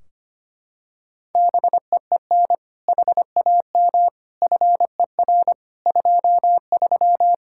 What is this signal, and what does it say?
Morse code, heard as a keyed tone, about 25 wpm, 700 Hz. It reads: BEEN HAM FER 23
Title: Practice QSO - Exchange 3 - More Info - 25wpm